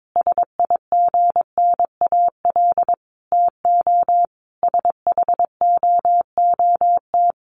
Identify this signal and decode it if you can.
Morse code, heard as a keyed tone, about 22 wpm, 700 Hz. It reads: SIGNAL TO SHOOT
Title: Sentences from Top 1400 Words 22wpm